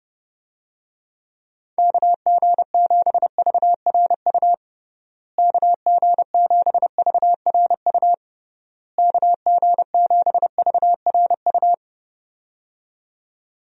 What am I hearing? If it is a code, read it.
Morse code, heard as a keyed tone, about 32 wpm, 700 Hz. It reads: KG7VRU KG7VRU KG7VRU
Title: USA Call Signs - 6 Characters - Repeated 3x - 30wpm